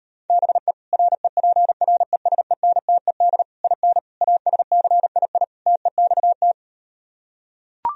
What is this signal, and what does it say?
Morse code, heard as a keyed tone, about 38 wpm, 700 Hz. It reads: BE REPRESENTED IN ASCII TEXT